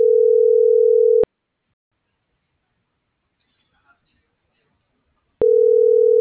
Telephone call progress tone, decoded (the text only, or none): ringback tone